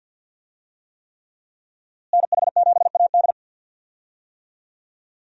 Morse code, heard as a keyed tone, about 46 wpm, 700 Hz. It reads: NH7AD
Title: USA Call Signs - 5 Characters - Repeated 3x - 50wpm